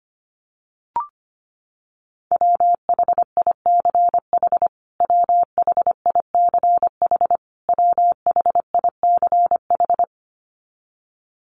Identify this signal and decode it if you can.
Morse code, heard as a keyed tone, about 25 wpm, 700 Hz. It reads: WHICH WHICH WHICH
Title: Top 500 Words Repeated 25wpm